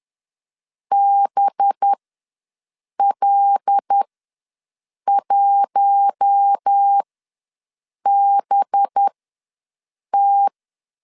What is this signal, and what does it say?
Morse code, heard as a keyed tone, about 10 wpm, spaced as for 6 wpm, 800 Hz. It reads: BL1BT
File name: LEZIONI / 5 LBVC / 5 LB ws 10wpm 3616